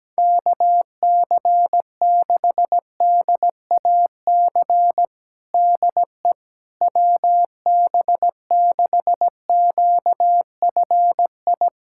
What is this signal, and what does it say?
Morse code, heard as a keyed tone, about 17 wpm, 700 Hz. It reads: KC6DAC DE WB6QFI
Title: USA Callsign DE Callsign - 6 Characters - 17wpm